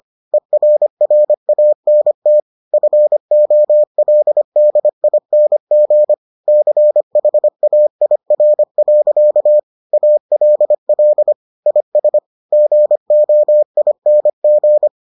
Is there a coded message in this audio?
Yes: Morse code from 0.3 to 14.9 s, heard as a keyed tone, about 25 wpm, 600 Hz. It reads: ERRANT FOLDING CHAIR. ALL IS GOING